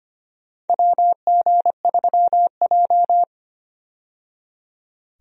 Morse code, heard as a keyed tone, about 25 wpm, 700 Hz. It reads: WG3J
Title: USA Call Signs - 4 Characters - 25wpm